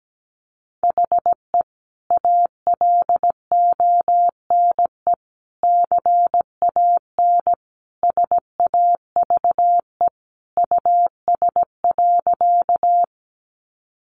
Morse code, heard as a keyed tone, about 17 wpm, 700 Hz. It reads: HE ALONE CAN SAVE US.
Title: Sentences from Top 1100 Words 17wpm